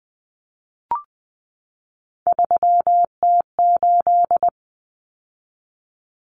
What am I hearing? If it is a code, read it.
Morse code, heard as a keyed tone, about 20 wpm, 700 Hz. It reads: 3T8
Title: Common Cut Numbers - 3-Digit 20wpm